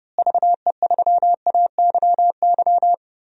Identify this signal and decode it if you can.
Morse code, heard as a keyed tone, about 30 wpm, 700 Hz. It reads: VE3AYY
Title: DXCC-100 Call Signs - 6 Characters - Repeated 3x - 30wpm